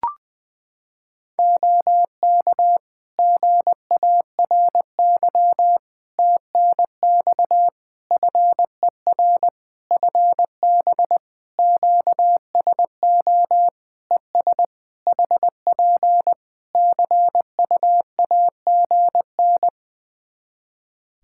Morse code, heard as a keyed tone, about 21 wpm, 700 Hz. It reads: OK GARY TNX FER FB QSO ES HP CUAGN
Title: Practice QSO - Exchange 4 - Ending - 20wpm